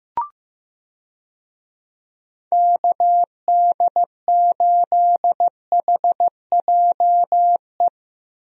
Morse code, heard as a keyed tone, about 15 wpm, 700 Hz. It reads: KD8HJE
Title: USA Call Signs - 6 Characters - Repeated 2x - 15wpm